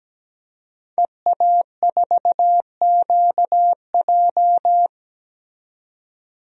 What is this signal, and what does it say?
Morse code, heard as a keyed tone, about 17 wpm, 700 Hz. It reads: EA4QJ
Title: DXCC-100 Call Signs - 5 Characters - 17wpm